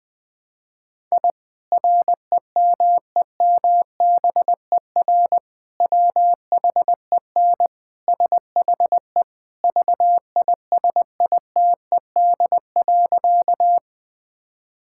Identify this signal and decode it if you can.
Morse code, heard as a keyed tone, about 21 wpm, 700 Hz. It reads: I REMEMBER WHEN SHE VISITED.
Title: Sentences with 2-Word Phrases - 20wpm